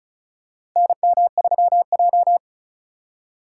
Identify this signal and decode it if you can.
Morse code, heard as a keyed tone, about 34 wpm, 700 Hz. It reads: NM3J
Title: USA Call Signs - 4 Characters - 4x Character Spacing - 35wpm